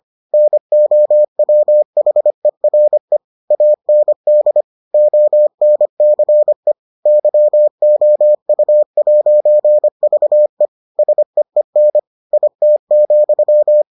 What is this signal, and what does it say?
Morse code, heard as a keyed tone, about 25 wpm, 600 Hz. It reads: NOWHERE AND ONCE YOU'VE SEEN IT,